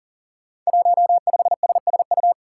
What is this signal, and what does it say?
Morse code, heard as a keyed tone, about 39 wpm, 700 Hz. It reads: 15SSU
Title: Five Challenging Letter-Number Combinations 40wpm